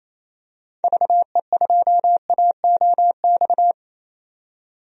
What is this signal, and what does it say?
Morse code, heard as a keyed tone, about 28 wpm, 700 Hz. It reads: VE2AOX
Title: DXCC-100 Call Signs - 6 Characters - 28wpm